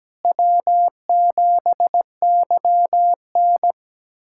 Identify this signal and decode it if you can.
Morse code, heard as a keyed tone, about 17 wpm, 700 Hz. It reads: W7YN
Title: USA Callsign DE Callsign - 4 Characters - 17wpm